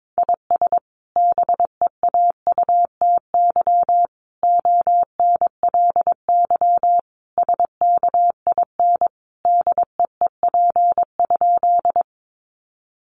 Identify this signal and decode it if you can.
Morse code, heard as a keyed tone, about 22 wpm, 700 Hz. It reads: IS BEAUTY ONLY SKIN DEEP?
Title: Sentences from Top 1300 Words 22wpm